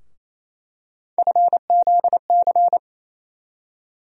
Morse code, heard as a keyed tone, about 28 wpm, 700 Hz. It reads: FZC
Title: USA Call Sign Postfixes - 3 Letters 28wpm